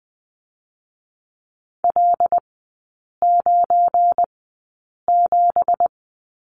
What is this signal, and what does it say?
Morse code, heard as a keyed tone, about 20 wpm, spaced as for 8 wpm, 700 Hz. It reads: L97
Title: 172 Lesson 28e - Number 9 - Three Characters 20wpm